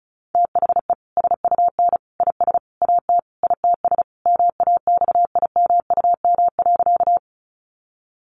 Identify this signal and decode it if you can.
Morse code, heard as a keyed tone, about 34 wpm, 700 Hz. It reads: THE SUN IS AT ITS MAXIMUM.
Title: Sentences from Top 1300 Words 35wpm